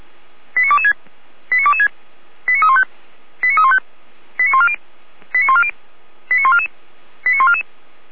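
Five-tone selective call (ZVEI), heard as quick runs of five tones, tones of about 70 ms. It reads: 89226 89226 89315 89315 89140 89140 89144 89144
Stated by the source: ZVEI five-tone call 89240 received by SDR, from audio.f85360.feuerwehr-orlamuende.org, fire brigade Orlamünde.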